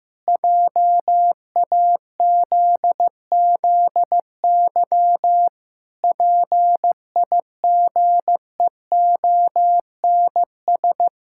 Morse code, heard as a keyed tone, about 15 wpm, 700 Hz. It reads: JAZZY PIGEONS
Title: Pangram Challenge v2 - 15wpm